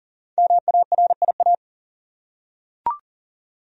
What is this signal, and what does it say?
Morse code, heard as a keyed tone, about 40 wpm, 700 Hz. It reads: MARIA
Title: USA Common Names - Max 5 Letters - 40wpm